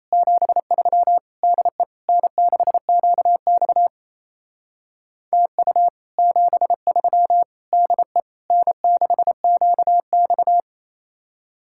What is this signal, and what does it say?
Morse code, heard as a keyed tone, about 33 wpm, 700 Hz. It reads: 73 DE N6QX TU 73 DE N6QX
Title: Speed-Racing Common QSO Phrases - 28wpm